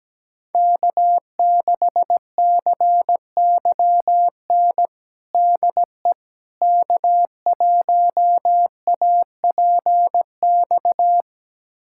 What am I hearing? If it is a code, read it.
Morse code, heard as a keyed tone, about 17 wpm, 700 Hz. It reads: K6CYN DE K1APX
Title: USA Callsign DE Callsign - 5 Characters - 17wpm